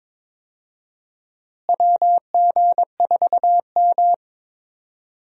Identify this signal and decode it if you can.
Morse code, heard as a keyed tone, about 22 wpm, 700 Hz. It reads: WG4M
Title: USA Call Signs - 4 Characters - Repeated 3x - 22wpm